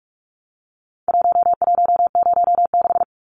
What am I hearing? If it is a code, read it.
Morse code, heard as a keyed tone, about 44 wpm, 700 Hz. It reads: 1106
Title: Numbers - Contest Style - 4-Digit 45wpm